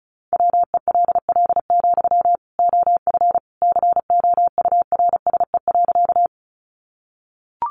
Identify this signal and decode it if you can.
Morse code, heard as a keyed tone, about 34 wpm, 700 Hz. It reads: WELL, OF COURSE.
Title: Sentences from Top 500 Words 35wpm